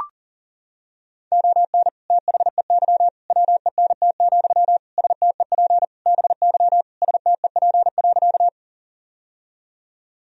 Morse code, heard as a keyed tone, about 40 wpm, 700 Hz. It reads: ON THEY WENT, STEP BY STEP.
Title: Sentences from Top 1000 Words - Review Part 2 - 40wpm